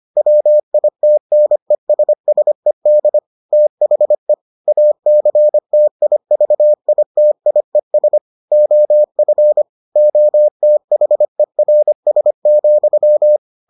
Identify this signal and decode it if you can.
Morse code, heard as a keyed tone, about 25 wpm, 600 Hz. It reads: WITNESSED THE ACTIVITIES OF OTHERS,